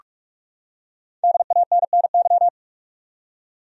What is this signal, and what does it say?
Morse code, heard as a keyed tone, about 46 wpm, 700 Hz. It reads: DANNY
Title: USA Common Names - 45wpm